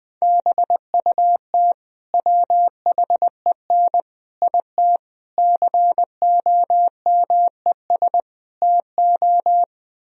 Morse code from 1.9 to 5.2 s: WHEN IT

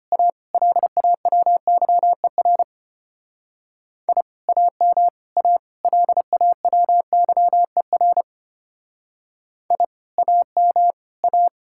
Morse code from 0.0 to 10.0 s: A LAWYER I AM A LAWYER I

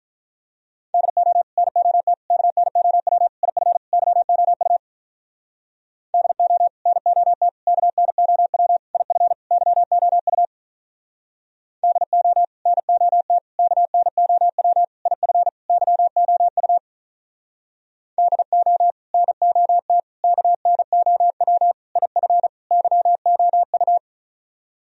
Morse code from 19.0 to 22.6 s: NOT KNOW IF

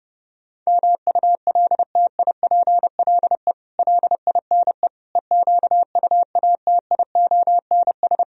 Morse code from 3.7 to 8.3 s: LINE EQUATIONS

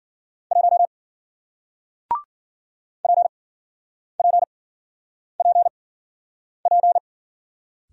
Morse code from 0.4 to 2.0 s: 1

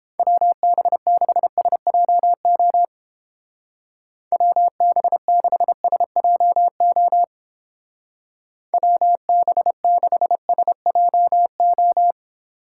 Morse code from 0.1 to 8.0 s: WB6SJO WB6SJO

